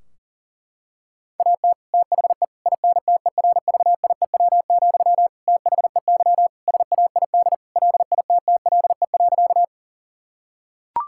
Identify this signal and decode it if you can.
Morse code, heard as a keyed tone, about 40 wpm, 700 Hz. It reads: AT THE INTERVIEW, THEY SAID LITTLE.
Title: Sentences from Top 1600 Words 40wpm